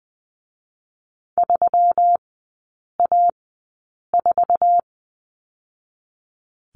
Morse code, heard as a keyed tone, about 20 wpm, spaced as for 8 wpm, 700 Hz. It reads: 3A4
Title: 121 Lesson 20e - Number 3 - Three Characters 20wpm